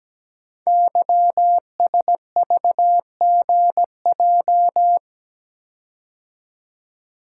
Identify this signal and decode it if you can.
Morse code, heard as a keyed tone, about 17 wpm, 700 Hz. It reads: YSVGJ
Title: Code Groups - 5 Letters - 17wpm